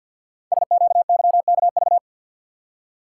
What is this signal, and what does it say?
Morse code, heard as a keyed tone, about 50 wpm, 700 Hz. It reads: SQYKV
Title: Code Groups - 5 Letters - 50wpm